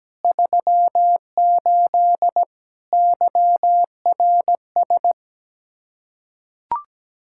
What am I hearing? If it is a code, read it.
Morse code, heard as a keyed tone, about 17 wpm, 700 Hz. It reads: 38 YRS
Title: K1USN SST QSO Practice 17wpm